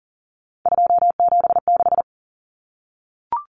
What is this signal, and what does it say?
Morse code, heard as a keyed tone, about 40 wpm, 700 Hz. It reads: 276
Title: Common Cut Numbers - 3-Digit 40wpm